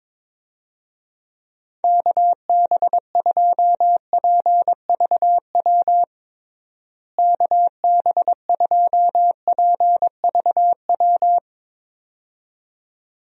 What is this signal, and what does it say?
Morse code, heard as a keyed tone, about 21 wpm, 700 Hz. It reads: KB2PVW KB2PVW
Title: USA Call Signs - 6 Characters - Repeated 2x - 22wpm